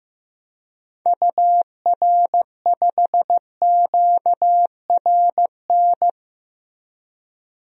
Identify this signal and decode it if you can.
Morse code, heard as a keyed tone, about 15 wpm, 700 Hz. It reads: UR5QRN
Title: DXCC-100 Call Signs - 6 Characters - 15wpm